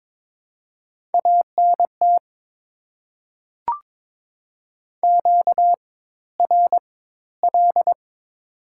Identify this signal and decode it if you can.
Morse code, heard as a keyed tone, about 22 wpm, spaced as for 14 wpm, 690 Hz. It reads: ANT QRL
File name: Top 20 QSO Elements - 4x Character Spacing - 22wpm